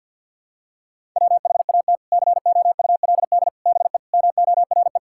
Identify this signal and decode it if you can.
Morse code, heard as a keyed tone, about 50 wpm, 700 Hz. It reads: WHAT COULD BE MORE
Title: Sentences from Top 1000 Words - Review Part 6 - 50wpm